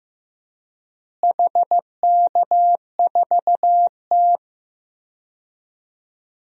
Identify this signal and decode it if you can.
Morse code, heard as a keyed tone, about 15 wpm, 700 Hz. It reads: HK4T
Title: DXCC-100 Call Signs - 4 Characters - Repeated 2x - 15wpm